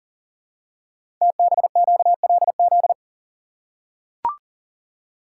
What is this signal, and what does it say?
Morse code, heard as a keyed tone, about 40 wpm, 700 Hz. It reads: TBQLZ
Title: Code Groups - 5 Letter:Numbers - 40wpm